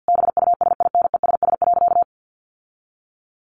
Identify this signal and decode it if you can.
Morse code, heard as a keyed tone, about 50 wpm, 700 Hz. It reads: BUSINESS.
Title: Sentences from Top 1400 Words 50wpm